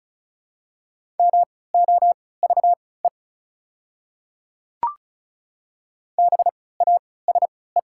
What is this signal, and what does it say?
Morse code, heard as a keyed tone, about 34 wpm, spaced as for 20 wpm, 690 Hz. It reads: MOVE BASE
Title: 3x Character Spacing - Top 300 Words - 35wpm